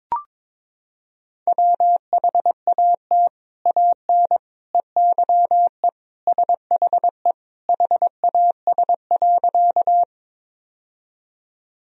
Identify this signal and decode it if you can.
Morse code, heard as a keyed tone, about 22 wpm, 700 Hz. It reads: WHAT AN EYE SHE HAS.
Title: Sentences from Top 1000 Words - Review Part 1 - 22wpm